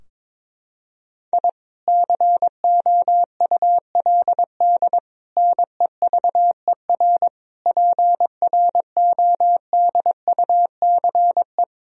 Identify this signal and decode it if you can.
Morse code, heard as a keyed tone, about 22 wpm, 700 Hz. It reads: I COULD NEVER PRODUCE
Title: Sentences from Top 1000 Words - Review Part 3 - 22wpm